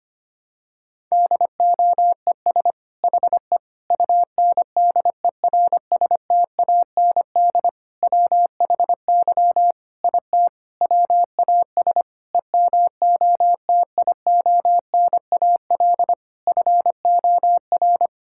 Morse code, heard as a keyed tone, about 25 wpm, 700 Hz. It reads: DOES HE UNDERSTAND WHY IT WAS EMOTIONAL FOR
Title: Sentences from Top 1000 Words - Review Part 5 - 25wpm